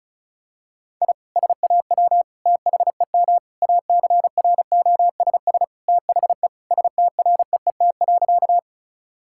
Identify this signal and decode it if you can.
Morse code, heard as a keyed tone, about 36 wpm, 700 Hz. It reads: I SAW THEM ACROSS THE STREET.
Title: Sentences from Top 1000 Words - Review Part 5 - 35wpm